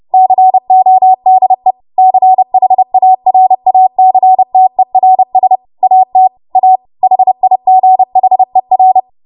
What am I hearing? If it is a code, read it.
Morse code, heard as a keyed tone, about 30 wpm, 750 Hz. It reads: CODE CHARACTERS AT A HIGHER
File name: Morse / 30WPM